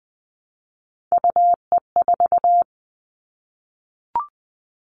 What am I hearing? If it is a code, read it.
Morse code, heard as a keyed tone, about 20 wpm, 700 Hz. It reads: UE4
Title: Three Challenging Letter-Number Combinations 20wpm